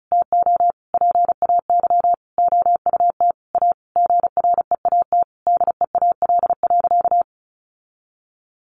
Morse code, heard as a keyed tone, about 34 wpm, 700 Hz. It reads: TO PAY OUT A GREAT DEAL.
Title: Sentences from Top 1500 Words 35wpm